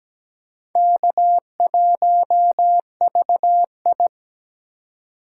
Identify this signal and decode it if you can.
Morse code, heard as a keyed tone, about 17 wpm, 700 Hz. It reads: K1VI
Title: USA Call Signs - 4 Characters - 4x Character Spacing - 17wpm